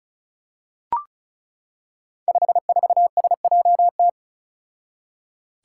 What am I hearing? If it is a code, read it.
Morse code, heard as a keyed tone, about 36 wpm, 700 Hz. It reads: 54SJT
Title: Five Challenging Letter-Number Combinations 35wpm